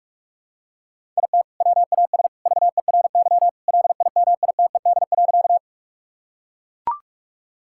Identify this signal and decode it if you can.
Morse code, heard as a keyed tone, about 41 wpm, 700 Hz. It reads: IT WAS VERY LIMITED.
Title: Sentences from Top 1000 Words - Review Part 6 - 45wpm